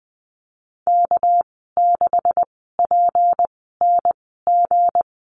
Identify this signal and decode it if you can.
Morse code, heard as a keyed tone, about 20 wpm, spaced as for 14 wpm, 700 Hz. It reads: K6PNG
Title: 231 Lesson 37g - Letter x - Call Signs 20wpm